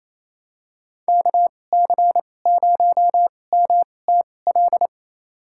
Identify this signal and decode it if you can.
Morse code, heard as a keyed tone, about 28 wpm, spaced as for 20 wpm, 690 Hz. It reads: KC0MTL
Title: USA Call Signs - 2x Character Spacing - 28wpm